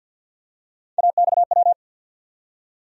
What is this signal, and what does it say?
Morse code, heard as a keyed tone, about 51 wpm, 700 Hz. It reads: AXW
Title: USA Call Sign Postfixes - 3 Letters 50wpm